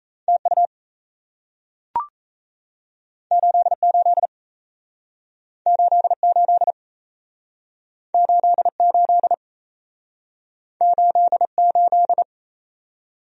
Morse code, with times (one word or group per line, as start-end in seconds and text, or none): 0.3-0.7 TU
3.3-4.3 88
5.7-6.7 88
8.1-9.3 88
10.8-12.2 88